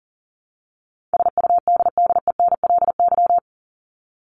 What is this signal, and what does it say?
Morse code, heard as a keyed tone, about 40 wpm, 700 Hz. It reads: SUDDENLY
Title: LY Suffix - Top 100 Words with Base Words 40wpm